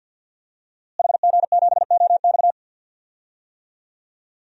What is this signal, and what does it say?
Morse code, heard as a keyed tone, about 53 wpm, 700 Hz. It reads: HG7OX
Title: DXCC-100 Call Signs - 5 Characters - Repeated 2x - 50wpm